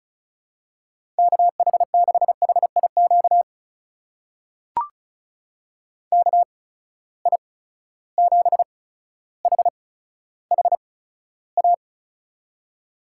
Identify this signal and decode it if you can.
Morse code, heard as a keyed tone, about 34 wpm, spaced as for 15 wpm, 700 Hz. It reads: KH6HIQ KI7HHA
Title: USA Call Signs - 6 Characters - 8x Character Spacing - 35wpm